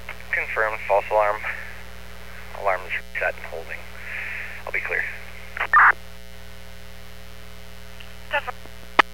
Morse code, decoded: none

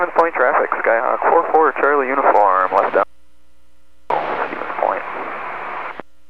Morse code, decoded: none

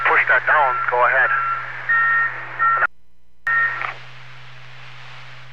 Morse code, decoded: none

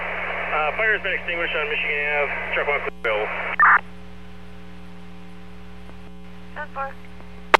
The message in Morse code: none

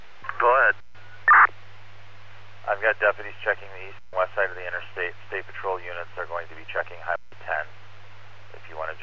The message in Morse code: none

